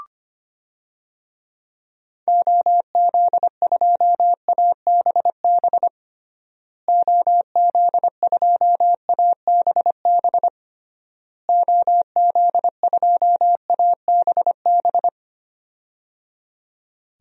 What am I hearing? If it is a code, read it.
Morse code, heard as a keyed tone, about 25 wpm, 700 Hz. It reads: OZ2ABB OZ2ABB OZ2ABB
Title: DXCC-100 Call Signs - 6 Characters - Repeated 3x - 25wpm